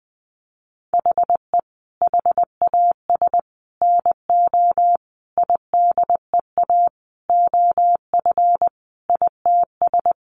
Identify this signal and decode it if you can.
Morse code, heard as a keyed tone, about 19 wpm, 700 Hz. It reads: HE HAS NO IDEA OF ITS
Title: Sentences from Top 1400 Words 20wpm